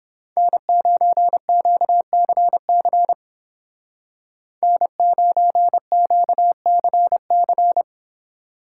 Morse code, heard as a keyed tone, about 30 wpm, 700 Hz. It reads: N9QCC N9QCC